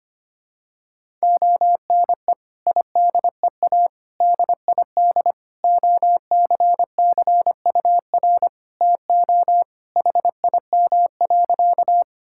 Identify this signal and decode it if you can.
Morse code, heard as a keyed tone, about 25 wpm, 700 Hz. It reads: ONE IDEA DID OCCUR TO HIM.
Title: Sentences from Top 1000 Words - Review Part 6 - 25wpm